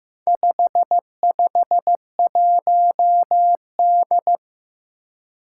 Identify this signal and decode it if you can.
Morse code, heard as a keyed tone, about 15 wpm, 700 Hz. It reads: 551D
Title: Four Challenging Letter-Number Combinations 15wpm